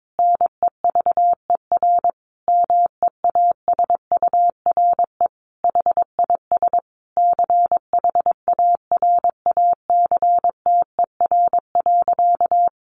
Morse code, heard as a keyed tone, about 22 wpm, 700 Hz. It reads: NEVER MEASURE HIS CHARACTER.
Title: Sentences from Top 1200 Words 22wpm